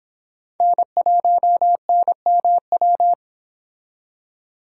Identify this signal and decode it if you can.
Morse code, heard as a keyed tone, about 30 wpm, 700 Hz. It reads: N1NMW